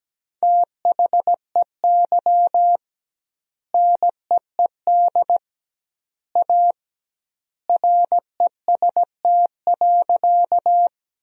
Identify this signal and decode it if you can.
Morse code, heard as a keyed tone, about 17 wpm, 700 Hz. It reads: THEY NEED A REST.
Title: 2x Word-Spacing - Sentences from Top 500 Words - 17wpm